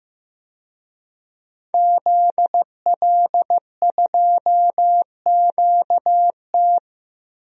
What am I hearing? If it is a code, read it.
Morse code, heard as a keyed tone, about 15 wpm, 700 Hz. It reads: ZL2QT
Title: DXCC-100 Call Signs - 5 Characters - Repeated 3x - 15wpm